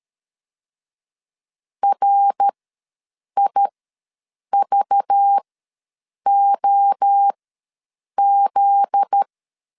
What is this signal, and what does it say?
Morse code, heard as a keyed tone, about 13 wpm, spaced as for 7 wpm, 800 Hz. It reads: RIVOZ
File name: LEZIONI / 7 GF9Z / 7 GF9Z ws 12wpm 3321